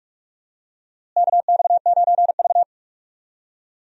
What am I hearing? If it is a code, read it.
Morse code, heard as a keyed tone, about 46 wpm, 700 Hz. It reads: KX9V
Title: USA Call Signs - 4 Characters - Repeated 2x - 45wpm